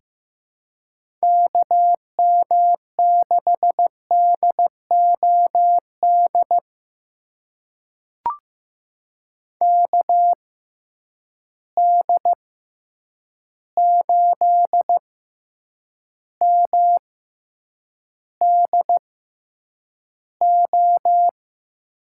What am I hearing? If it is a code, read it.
Morse code, heard as a keyed tone, about 15 wpm, spaced as for 8 wpm, 700 Hz. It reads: KM6DOD KD8MDO
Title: USA Call Signs - 6 Characters - 6x Character Spacing - 15wpm